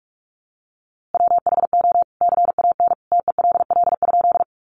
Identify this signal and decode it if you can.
Morse code, heard as a keyed tone, about 48 wpm, 700 Hz. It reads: WHO CAN TELL?
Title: A Sound Of Thunder 45wpm